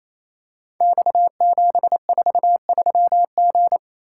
Morse code, heard as a keyed tone, about 28 wpm, 700 Hz. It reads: X743G
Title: Code Groups - 5 Letter:Numbers - 28wpm